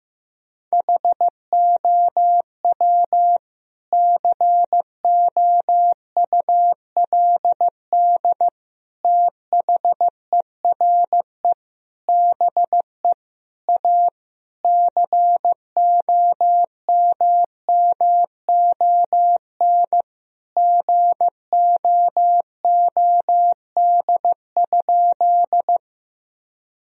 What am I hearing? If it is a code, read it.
Morse code, heard as a keyed tone, about 15 wpm, 700 Hz. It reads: HOW COULD THERE BE A COMMON GOOD?
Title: Sentences from Top 1000 Words - Review Part 3 - 15wpm